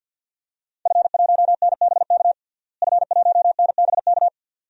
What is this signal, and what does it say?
Morse code, heard as a keyed tone, about 46 wpm, 700 Hz. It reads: F1NBK F1NBK
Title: DXCC-100 Call Signs - 5 Characters - Repeated 2x - 50wpm